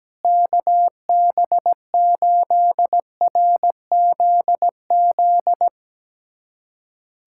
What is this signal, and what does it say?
Morse code, heard as a keyed tone, about 17 wpm, 700 Hz. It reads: KB8RZZ
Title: USA Callsign DE Callsign 17wpm